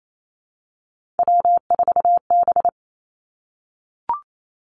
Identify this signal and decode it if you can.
Morse code, heard as a keyed tone, about 28 wpm, 700 Hz. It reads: W4B
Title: Three Challenging Letter-Number Combinations 28wpm